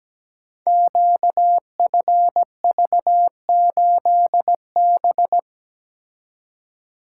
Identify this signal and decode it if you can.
Morse code, heard as a keyed tone, about 17 wpm, 700 Hz. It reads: QFV8B
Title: Code Groups - 5 Letter:Numbers - 17wpm